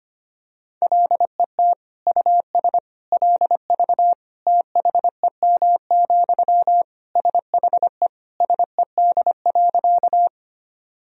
Morse code, heard as a keyed tone, about 25 wpm, 700 Hz. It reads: LET US LV THEM, SHE SED.
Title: Sentences from Top 400 Words and Abbreviations 25wpm